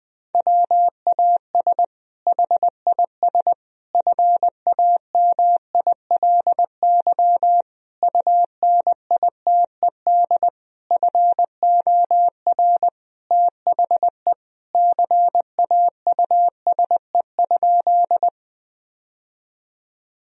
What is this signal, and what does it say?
Morse code, heard as a keyed tone, about 20 wpm, 700 Hz. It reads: WAS HIS FAMILY UNITED FOR THE CAUSE?
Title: Sentences from Top 1000 Words - Review Part 4 - 20wpm